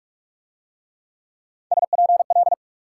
Morse code, heard as a keyed tone, about 41 wpm, 700 Hz. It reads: SPL